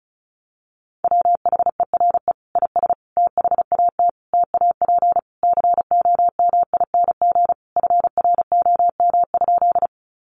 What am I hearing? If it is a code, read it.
Morse code, heard as a keyed tone, about 34 wpm, 700 Hz. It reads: WHERE IS THAT TAP COMING FROM?
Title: Sentences from Top 1400 Words 35wpm